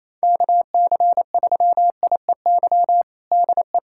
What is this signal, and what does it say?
Morse code, heard as a keyed tone, about 28 wpm, 700 Hz. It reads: KC3IEY DE
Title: USA Callsign DE Callsign 28wpm